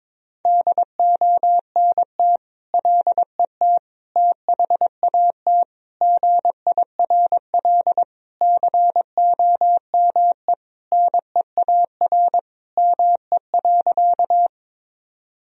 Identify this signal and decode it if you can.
Morse code, heard as a keyed tone, about 22 wpm, 700 Hz. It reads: DONT LET THAT GIRL COME NEAR ME.
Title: Sentences from Top 1000 Words - Review Part 1 - 22wpm